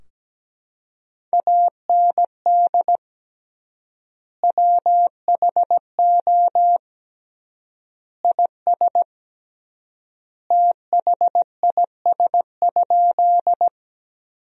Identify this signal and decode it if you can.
Morse code, heard as a keyed tone, about 17 wpm, 700 Hz. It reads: AND WHO IS THIS?
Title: 3x Word-Spacing - Sentences from Top 100 Words - 17wpm